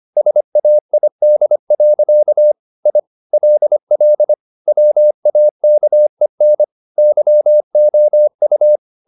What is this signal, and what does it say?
Morse code, heard as a keyed tone, about 25 wpm, 600 Hz. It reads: SAID. I LL WAKEN YOU
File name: the old man and the sea part 1 25wpm